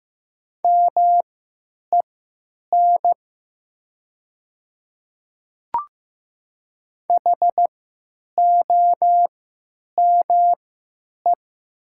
Morse code, heard as a keyed tone, about 15 wpm, spaced as for 9 wpm, 700 Hz. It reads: MEN HOME
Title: 3x Character Spacing - Top 300 Words - 15wpm